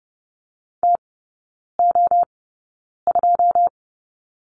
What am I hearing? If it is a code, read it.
Morse code, heard as a keyed tone, about 30 wpm, spaced as for 9 wpm, 700 Hz. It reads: TO2
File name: Lesson 16e - Letter c - Three Characters 30wpm